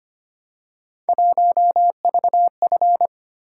Morse code, heard as a keyed tone, about 25 wpm, 700 Hz. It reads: 1VF
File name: Three Letter-Number Combinations - 25wpm